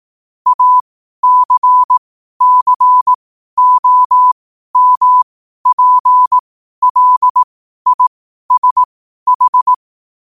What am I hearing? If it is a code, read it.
Morse code, heard as a keyed tone, about 18 wpm, spaced as for 12 wpm, 1000 Hz. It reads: ACCOMPLISH
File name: qotdfiles / 12wpm